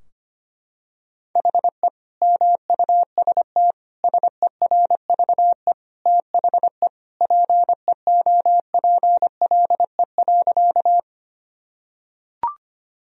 Morse code, heard as a keyed tone, about 25 wpm, 700 Hz. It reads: HE MUST SERVE THE PEOPLE.
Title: Sentences from Top 1000 Words - Review Part 2 - 25wpm